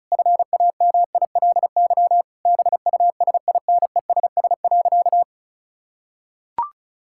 Morse code, heard as a keyed tone, about 36 wpm, 700 Hz. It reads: FAMILY BUSINESS.
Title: Sentences from Top 1000 Words - Review Part 6 - 35wpm